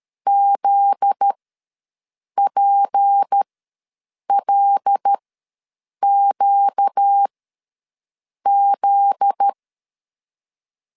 Morse code, heard as a keyed tone, about 13 wpm, spaced as for 6 wpm, 800 Hz. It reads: ZPLQZ
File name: LEZIONI / 7 GF9Z / 7 GF9Z ws 12wpm 3321